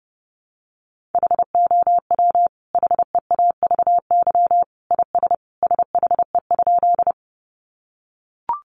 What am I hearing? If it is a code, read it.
Morse code, heard as a keyed tone, about 30 wpm, 700 Hz. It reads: HOW HEAVY IS SHE?
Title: Sentences from Top 1100 Words 30wpm